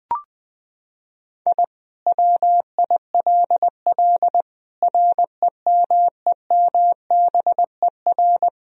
Morse code, heard as a keyed tone, about 21 wpm, 700 Hz. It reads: I WILL REMEMBER
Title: Sentences from Top 1000 Words - Review Part 2 - 20wpm